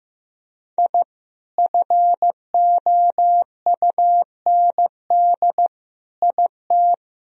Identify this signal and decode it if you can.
Morse code, heard as a keyed tone, about 15 wpm, 700 Hz. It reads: I FOUND IT
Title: Sentences from Top 1000 Words - Review Part 1 - 15wpm